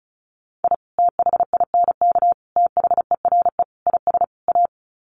Morse code, heard as a keyed tone, about 34 wpm, 700 Hz. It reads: I THINK THERE IS A
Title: Sets of 5 Words 35wpm